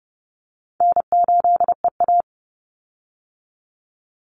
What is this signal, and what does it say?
Morse code, heard as a keyed tone, about 30 wpm, 700 Hz. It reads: N8EA
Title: Call Signs - 30wpm